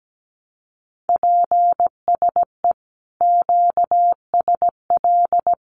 Morse code, heard as a keyed tone, about 17 wpm, 700 Hz. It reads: PSE QSL
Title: Common QSO Phrases Encore 17wpm